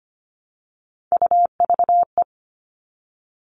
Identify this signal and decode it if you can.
Morse code, heard as a keyed tone, about 25 wpm, 700 Hz. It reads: UVE
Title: All Cut Numbers - 3-Digit 25wpm